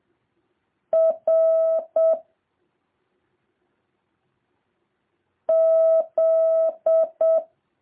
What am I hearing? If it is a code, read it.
Morse code, heard as a keyed tone, about 7 wpm, 650 Hz. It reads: R Z